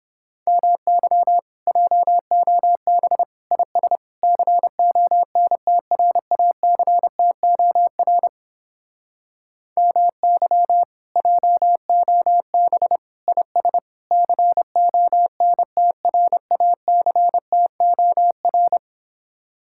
Morse code, heard as a keyed tone, about 31 wpm, 700 Hz. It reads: MY JOB IS CONTRACTOR MY JOB IS CONTRACTOR